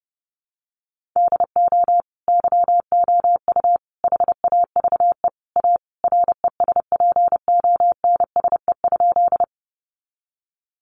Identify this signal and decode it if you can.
Morse code, heard as a keyed tone, about 30 wpm, 700 Hz. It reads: DO YOU HAVE A RESPONSE?
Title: Sentences from Top 1400 Words 30wpm